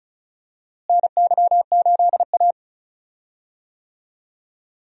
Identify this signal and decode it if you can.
Morse code, heard as a keyed tone, about 36 wpm, 700 Hz. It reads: NY8A
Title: USA Call Signs - 4 Characters - Repeated 3x - 35wpm